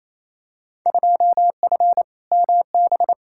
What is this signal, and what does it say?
Morse code, heard as a keyed tone, about 28 wpm, 700 Hz. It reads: 2F MB
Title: Field Day Exchanges Encore - 28wpm